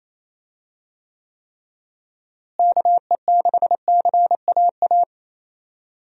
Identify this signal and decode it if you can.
Morse code, heard as a keyed tone, about 27 wpm, 700 Hz. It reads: KE6CAA
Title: USA Call Signs - 6 Characters - Repeated 3x - 28wpm